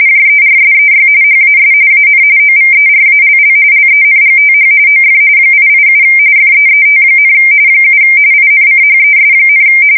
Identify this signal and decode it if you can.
RTTY, about 45.45 baud. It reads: JERSEY HAS AN OVERALL LENGTH OF 887 FEET, 7 INCHES,
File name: fsk170 / battleshipnj